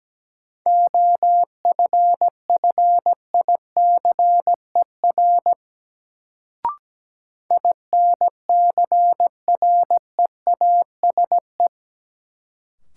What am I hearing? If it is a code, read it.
Morse code, heard as a keyed tone, about 17 wpm, 700 Hz. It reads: OFFICER INCREASE
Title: Sentences from Top 1200 Words - New Word Drill Alt - 17wpm